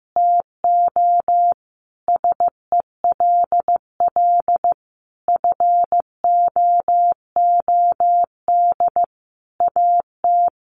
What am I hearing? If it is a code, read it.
Morse code, heard as a keyed tone, about 15 wpm, 700 Hz. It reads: TO SELL FOOD AT
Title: Sentences from Top 1100 Words 15wpm